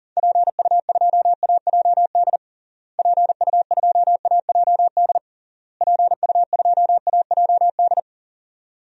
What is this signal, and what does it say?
Morse code, heard as a keyed tone, about 40 wpm, 700 Hz. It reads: PU2AJD PU2AJD PU2AJD
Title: DXCC-100 Call Signs - 6 Characters - Repeated 3x - 40wpm